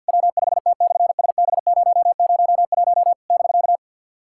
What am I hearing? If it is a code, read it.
Morse code, heard as a keyed tone, about 50 wpm, 700 Hz. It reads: W5T/SB001 <BK>
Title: SOTA QSO Practice 50wpm